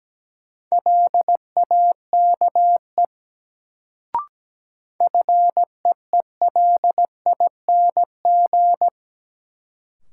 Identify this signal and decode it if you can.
Morse code, heard as a keyed tone, about 17 wpm, 700 Hz. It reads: LAKE FEELING
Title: Sentences from Top 1000 Words - Review Part 3 - New Word Drill - 17wpm